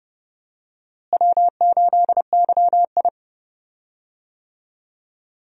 Morse code, heard as a keyed tone, about 30 wpm, 700 Hz. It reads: W8YI
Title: USA Call Signs - 4 Characters - Repeated 3x - 30wpm